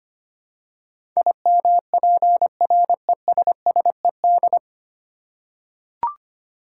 Morse code, heard as a keyed tone, about 25 wpm, 700 Hz. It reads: IMPRESSED
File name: practice / 25wpm / 9 Letter Words - First 3 Letters Given - 25wpm